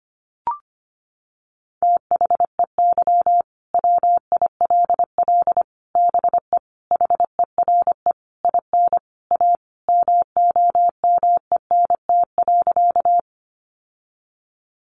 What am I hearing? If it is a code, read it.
Morse code, heard as a keyed tone, about 25 wpm, 700 Hz. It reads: THEY WILL BE HERE IN A MOMENT.
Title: Sentences from Top 1000 Words 25wpm